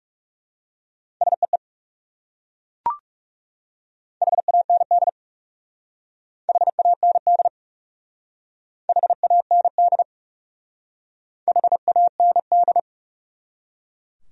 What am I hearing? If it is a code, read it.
Morse code, heard as a keyed tone, about 37 wpm, 700 Hz. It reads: SEE HAND HAND HAND HAND
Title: Speed-Racing Top 500 Words 30wpm